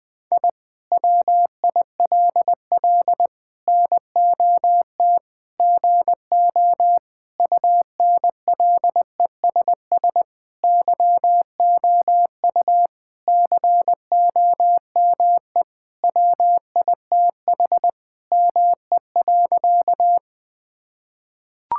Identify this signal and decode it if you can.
Morse code, heard as a keyed tone, about 20 wpm, 700 Hz. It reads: I WILL NOT GO UNLESS YOU COME WITH ME.
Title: Sentences from Top 1000 Words - Review Part 6 - 20wpm